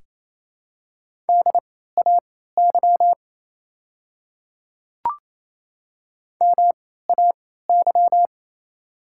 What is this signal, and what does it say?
Morse code, heard as a keyed tone, about 28 wpm, spaced as for 16 wpm, 700 Hz. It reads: DAY MAY
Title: 3x Character Spacing - Top 100 Words - 28wpm